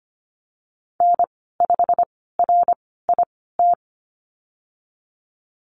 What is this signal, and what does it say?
Morse code, heard as a keyed tone, about 25 wpm, spaced as for 16 wpm, 700 Hz. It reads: N5RIT
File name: Lesson 15g - Number 5 - Call Signs 25wpm